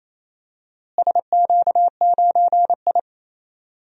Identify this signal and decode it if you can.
Morse code, heard as a keyed tone, about 28 wpm, 700 Hz. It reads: SQ9I
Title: DXCC-100 Call Signs - 4 Characters - Repeated 3x - 28wpm